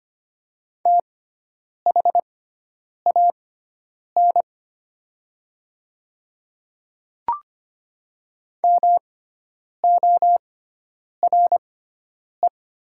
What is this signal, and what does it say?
Morse code, heard as a keyed tone, about 25 wpm, spaced as for 9 wpm, 700 Hz. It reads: THAN MORE
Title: 6x Character Spacing - Top 100 Words - 25wpm